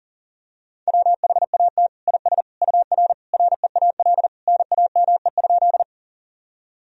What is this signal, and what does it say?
Morse code, heard as a keyed tone, about 39 wpm, 700 Hz. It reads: WHAT IS UR REAL NAME?
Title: Sentences from Top 300 Words and Abbreviations 40wpm